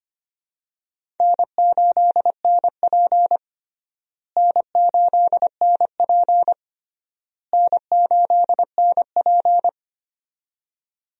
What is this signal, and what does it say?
Morse code, heard as a keyed tone, about 25 wpm, 700 Hz. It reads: N8NP N8NP N8NP
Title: USA Call Signs - 4 Characters - Repeated 3x - 25wpm